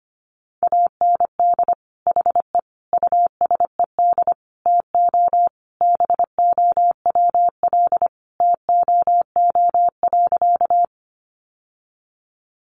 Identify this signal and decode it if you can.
Morse code, heard as a keyed tone, about 25 wpm, 700 Hz. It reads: AND HE USED TO BOWL TOO.
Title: Sentences from Top 1200 Words 25wpm